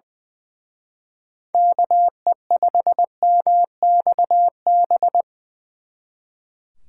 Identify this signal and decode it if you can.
Morse code, heard as a keyed tone, about 24 wpm, 700 Hz. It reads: KE5MXB
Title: Speed Racing - USA 2x3 Call Signs 20wpm